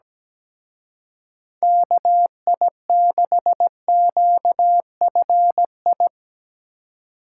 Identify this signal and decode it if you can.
Morse code, heard as a keyed tone, about 19 wpm, 700 Hz. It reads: KI6QFI
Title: Speed Racing - USA 2x3 Call Signs 17wpm